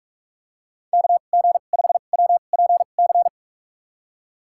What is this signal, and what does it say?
Morse code, heard as a keyed tone, about 47 wpm, spaced as for 33 wpm, 700 Hz. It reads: KG5WPC
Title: USA Call Signs - 2x Character Spacing - 45wpm